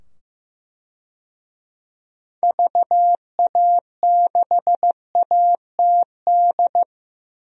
Morse code, heard as a keyed tone, about 15 wpm, 700 Hz. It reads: VA6ATD
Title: DXCC-100 Call Signs - 6 Characters - 15wpm